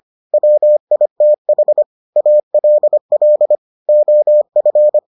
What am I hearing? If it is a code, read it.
Morse code, heard as a keyed tone, about 25 wpm, 600 Hz. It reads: WITH ALL OF